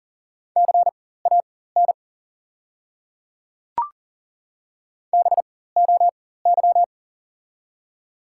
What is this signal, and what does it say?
Morse code, heard as a keyed tone, about 40 wpm, spaced as for 19 wpm, 690 Hz. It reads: CAN BOY
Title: 4x Character Spacing - Top 200 Words - 40wpm